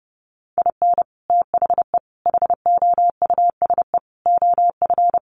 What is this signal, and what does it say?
Morse code, heard as a keyed tone, about 30 wpm, 700 Hz. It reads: IN THE HOUSE OF
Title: Sets of 5 Words Encore 30wpm